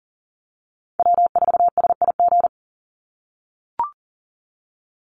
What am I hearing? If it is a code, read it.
Morse code, heard as a keyed tone, about 40 wpm, 700 Hz. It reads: W4SIG
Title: Call Signs - 40wpm